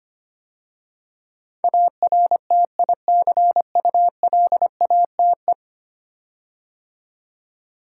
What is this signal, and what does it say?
Morse code, heard as a keyed tone, about 25 wpm, 700 Hz. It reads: ARTICULATE
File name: 10 Letter Words - First 3 Letters Given - 25wpm